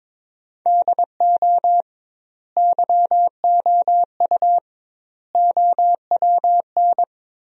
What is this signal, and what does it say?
Morse code, heard as a keyed tone, about 22 wpm, 700 Hz. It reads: DO YOU OWN
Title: 2x Word-Spacing - Sentences from Top 400 Words - 22wpm